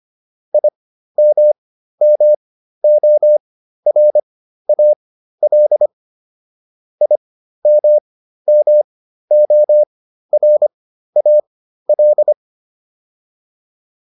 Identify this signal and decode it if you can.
Morse code, heard as a keyed tone, about 25 wpm, spaced as for 13 wpm, 600 Hz. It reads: IMMORAL IMMORAL